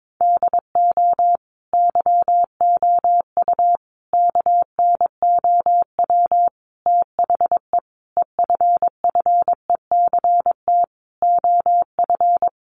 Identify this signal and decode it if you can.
Morse code, heard as a keyed tone, about 22 wpm, 700 Hz. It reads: DO YOU KNOW THE EFFECT OF
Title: Sentences from Top 1400 Words 22wpm